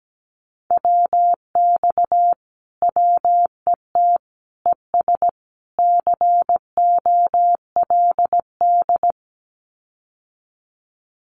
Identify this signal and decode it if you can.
Morse code, heard as a keyed tone, about 17 wpm, 700 Hz. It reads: WX WET ES COLD
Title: Common QSO Phrases Encore 17wpm